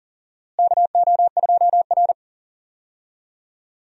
Morse code, heard as a keyed tone, about 40 wpm, 700 Hz. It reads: KO2R
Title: USA Call Signs - 4 Characters - Repeated 3x - 40wpm